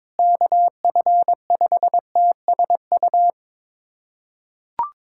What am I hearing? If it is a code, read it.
Morse code, heard as a keyed tone, about 22 wpm, 700 Hz. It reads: KF5TSU
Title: USA Callsign DE Callsign - 6 Characters - 22wpm